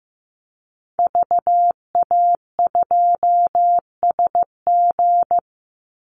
Morse code, heard as a keyed tone, about 15 wpm, 700 Hz. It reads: VA2SG
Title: Call Signs Repeated 2x - 15wpm